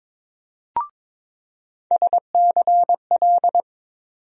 Morse code, heard as a keyed tone, about 22 wpm, 700 Hz. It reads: SCL
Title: USA Call Sign Postfixes - 3 Letters 22wpm